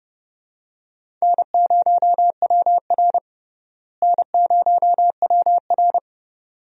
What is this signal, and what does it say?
Morse code, heard as a keyed tone, about 30 wpm, 700 Hz. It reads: N0WR N0WR
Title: USA Call Signs - 4 Characters - Repeated 2x - 30wpm